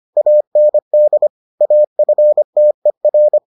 Morse code, heard as a keyed tone, about 25 wpm, 600 Hz. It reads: AND AFTER